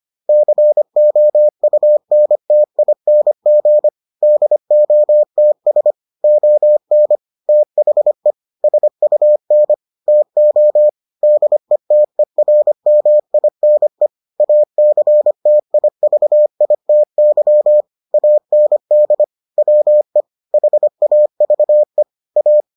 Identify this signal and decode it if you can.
Morse code, heard as a keyed tone, about 25 wpm, 600 Hz. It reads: COUNTING DOTS ON THE SUN TO DETERMINE ACTIVITY AND WE HAVE A